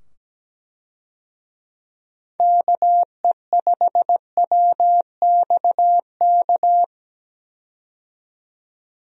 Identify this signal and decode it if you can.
Morse code, heard as a keyed tone, about 17 wpm, 700 Hz. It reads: KE5WXK
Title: USA Call Signs - 6 Characters - 17wpm